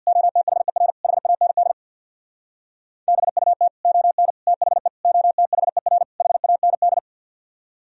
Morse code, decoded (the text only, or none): OTHER HAND BUT ON THE OTHER HAND